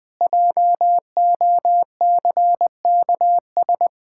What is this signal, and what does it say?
Morse code, heard as a keyed tone, about 20 wpm, 700 Hz. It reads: JOCKS
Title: Pangram Challenge v2 - 20wpm